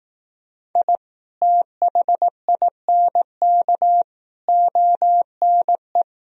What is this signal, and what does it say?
Morse code, heard as a keyed tone, about 18 wpm, 700 Hz. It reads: I THINK ONE
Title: Speed-Racing - Sets of 5 Words 15wpm